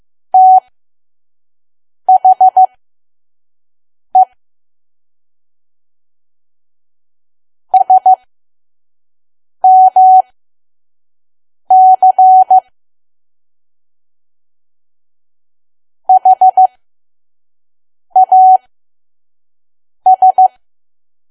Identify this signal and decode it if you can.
Morse code, heard as a keyed tone, about 15 wpm, spaced as for 5 wpm, 750 Hz. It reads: THE SMC HAS
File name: Archive / 5 WPM files / PM